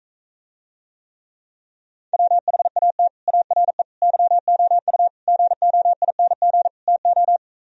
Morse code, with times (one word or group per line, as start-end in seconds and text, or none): 2.1-3.1 WHAT
3.3-3.8 ARE
4.0-5.1 YOU
5.3-6.7 GOING
6.9-7.4 TO